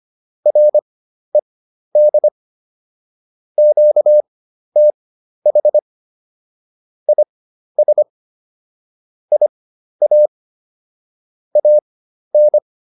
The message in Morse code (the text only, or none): RED QTH IS IA AN